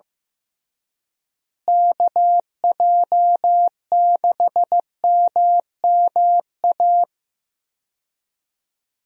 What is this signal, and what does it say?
Morse code, heard as a keyed tone, about 15 wpm, 700 Hz. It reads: KJ6MMA
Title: USA Call Signs - 6 Characters - Repeated 3x - 15wpm